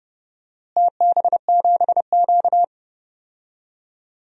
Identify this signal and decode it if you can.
Morse code, heard as a keyed tone, about 32 wpm, 700 Hz. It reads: TB7Q
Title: DXCC-100 Call Signs - 4 Characters - Repeated 2x - 30wpm